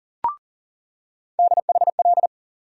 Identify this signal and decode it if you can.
Morse code, heard as a keyed tone, about 40 wpm, 700 Hz. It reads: DHL
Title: USA Call Sign Postfixes - 3 Letters 40wpm